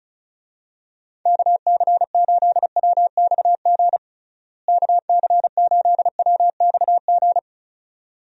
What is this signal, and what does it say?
Morse code, heard as a keyed tone, about 36 wpm, 700 Hz. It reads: KC8WXG KC8WXG
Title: USA Call Signs - 6 Characters - Repeated 2x - 35wpm